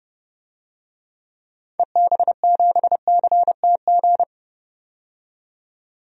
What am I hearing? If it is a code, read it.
Morse code, heard as a keyed tone, about 31 wpm, 700 Hz. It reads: EB7CTG
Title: DXCC-100 Call Signs - 6 Characters - 30wpm